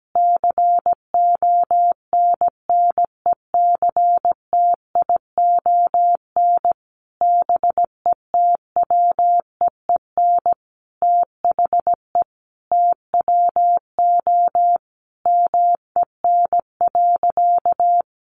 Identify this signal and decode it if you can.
Morse code, heard as a keyed tone, about 17 wpm, 700 Hz. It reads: CONNECTION BETWEEN THE TWO MEN.
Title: Sentences from Top 1300 Words 17wpm